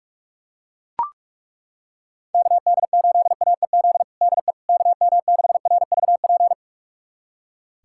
Morse code, heard as a keyed tone, about 41 wpm, 700 Hz. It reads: KD8AEZ DE KM6RVP
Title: USA Callsign DE Callsign - 6 Characters - 45wpm